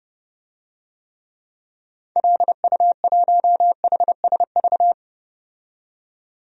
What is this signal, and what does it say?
Morse code, heard as a keyed tone, about 29 wpm, 700 Hz. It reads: LU1HSV
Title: DXCC-100 Call Signs - 6 Characters - 30wpm